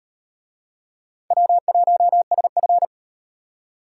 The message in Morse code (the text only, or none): W1SF